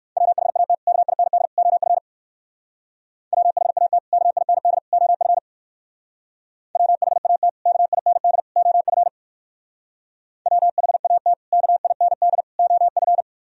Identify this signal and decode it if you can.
Morse code, heard as a keyed tone, about 58 wpm, 700 Hz. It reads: WHAT KIND OF WHAT KIND OF WHAT KIND OF WHAT KIND OF